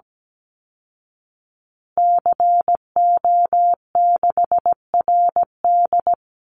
Morse code, heard as a keyed tone, about 17 wpm, 700 Hz. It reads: CO6RD
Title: Call Signs Repeated 2x - 17wpm